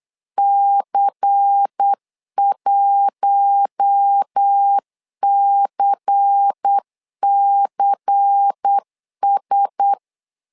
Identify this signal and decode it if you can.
Morse code, heard as a keyed tone, about 8 wpm, 800 Hz. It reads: C1CCS
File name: LEZIONI / 5 LBVC / 5 LBVC 8wpm 2935